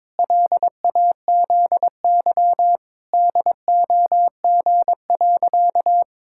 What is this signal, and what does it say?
Morse code, heard as a keyed tone, about 22 wpm, 700 Hz. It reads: LAZY DOG.
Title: Pangram Challenge v2 - 22wpm